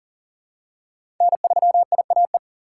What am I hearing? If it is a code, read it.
Morse code, heard as a keyed tone, about 37 wpm, 700 Hz. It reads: N3IAE
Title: USA Call Signs - 5 Characters - 4x Character Spacing - 40wpm